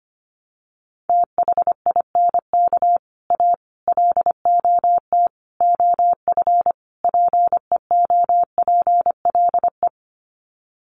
Morse code, heard as a keyed tone, about 25 wpm, 700 Hz. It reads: THINK A LOT OF PEOPLE
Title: Sets of 5 Words 25wpm